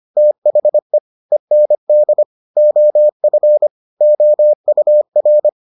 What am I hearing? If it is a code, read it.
Morse code, heard as a keyed tone, about 25 wpm, 600 Hz. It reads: THE END OF OUR